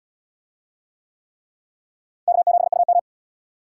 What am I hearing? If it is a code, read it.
Morse code, heard as a keyed tone, about 72 wpm, 700 Hz. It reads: K7UM